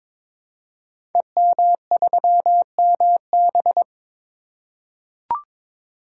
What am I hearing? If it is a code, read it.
Morse code, heard as a keyed tone, about 22 wpm, 690 Hz. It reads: EM3MB
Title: Code Groups - 5 Letter:Numbers - 22wpm